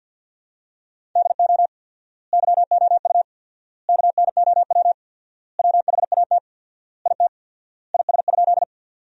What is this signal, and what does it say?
Morse code, heard as a keyed tone, about 47 wpm, 700 Hz. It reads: DO YOU KNOW WHAT IT IS?
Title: 4x Word-Spacing - Sentences from Top 100 Words - 50wpm